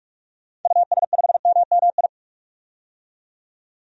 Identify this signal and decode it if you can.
Morse code, heard as a keyed tone, about 46 wpm, 700 Hz. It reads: US5MMI
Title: DXCC-100 Call Signs - 6 Characters - 45wpm